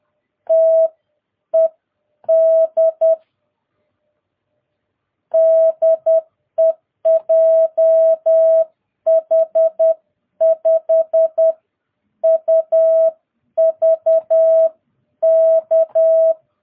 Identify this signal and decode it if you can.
Morse code, heard as a keyed tone, about 10 wpm, spaced as for 8 wpm, 660 Hz. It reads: TED DE JH5UVK
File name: jh8cbh / 2022cw